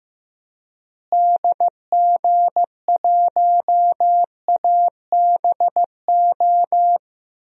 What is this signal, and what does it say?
Morse code, heard as a keyed tone, about 15 wpm, 700 Hz. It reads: DG1ABO
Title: DXCC-100 Call Signs - 6 Characters - Repeated 3x - 15wpm